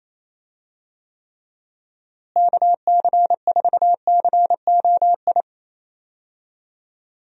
Morse code, heard as a keyed tone, about 28 wpm, 700 Hz. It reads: KC4COI
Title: USA Call Signs - 6 Characters - Repeated 2x - 28wpm